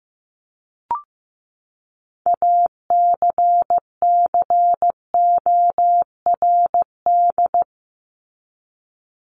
Morse code, heard as a keyed tone, about 15 wpm, 700 Hz. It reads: ACCORD
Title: ING Suffix - Top 100 Words with Base Words 15wpm